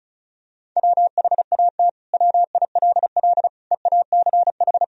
Morse code, heard as a keyed tone, about 34 wpm, 690 Hz. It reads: WHAT WILL EACH
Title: Sentences from Top 200 Words and Abbreviations 35wpm